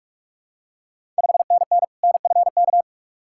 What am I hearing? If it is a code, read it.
Morse code, heard as a keyed tone, about 44 wpm, 690 Hz. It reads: 5NN NFK
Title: Ontario QSO Party Exchange 45wpm